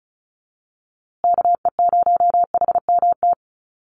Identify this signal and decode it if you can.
Morse code, heard as a keyed tone, about 34 wpm, 700 Hz. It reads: KE0HMT
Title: USA Call Signs Repeated 2x 35wpm